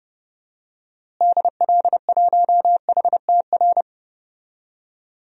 Morse code, heard as a keyed tone, about 30 wpm, 700 Hz. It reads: DL1HTR
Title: DXCC-100 Call Signs - 6 Characters - Repeated 2x - 30wpm